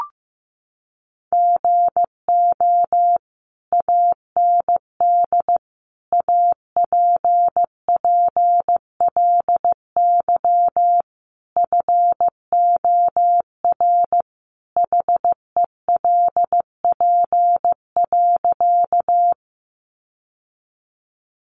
Morse code, heard as a keyed tone, about 15 wpm, 700 Hz. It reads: GO AND APPLY FOR HELP.
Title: Sentences from Top 1200 Words 15wpm